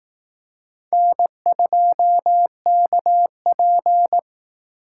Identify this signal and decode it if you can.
Morse code, heard as a keyed tone, about 18 wpm, 700 Hz. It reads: N2KP
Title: Speed Racing - USA 1x2 Call Signs 15wpm